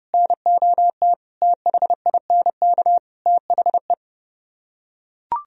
Morse code, heard as a keyed tone, about 28 wpm, 700 Hz. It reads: NOT THINK THE